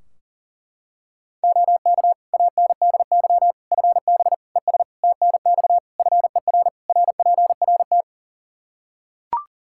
Morse code, heard as a keyed tone, about 40 wpm, 700 Hz. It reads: OK ANDY FB ES TNX FER RPRT
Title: Practice QSO - Exchange 2 - Info - 40wpm